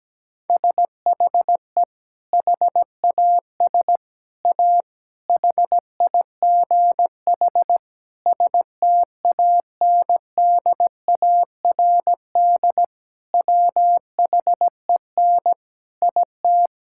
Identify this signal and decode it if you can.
Morse code, heard as a keyed tone, about 17 wpm, 700 Hz. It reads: SHE HAS A HIGH STANDARD WHEN IT
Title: Sentences from Top 1000 Words - Review Part 6 - 17wpm